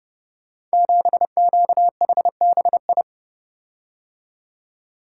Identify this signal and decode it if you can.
Morse code, heard as a keyed tone, about 30 wpm, 700 Hz. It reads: 7QHBI
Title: Code Groups - 5 Letter:Numbers - 30wpm